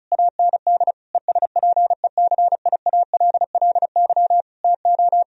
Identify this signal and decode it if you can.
Morse code, heard as a keyed tone, about 36 wpm, 700 Hz. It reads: AND ESPECIALLY TO
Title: Sentences from Top 1000 Words - Review Part 4 - 35wpm